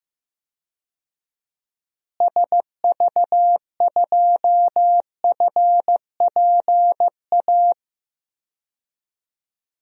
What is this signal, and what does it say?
Morse code, heard as a keyed tone, about 15 wpm, 700 Hz. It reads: SV2FPA
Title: DXCC-100 Call Signs - 6 Characters - Repeated 3x - 15wpm